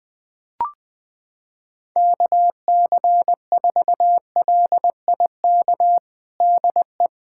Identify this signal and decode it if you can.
Morse code, heard as a keyed tone, about 20 wpm, 700 Hz. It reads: KC4LIK DE
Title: USA Callsign DE Callsign - 6 Characters - 20wpm